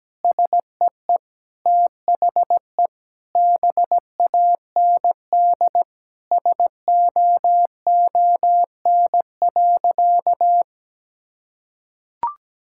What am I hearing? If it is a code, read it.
Morse code, heard as a keyed tone, about 17 wpm, 700 Hz. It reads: SEE THE BAND SOON.
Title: Sentences from Top 1600 Words 17wpm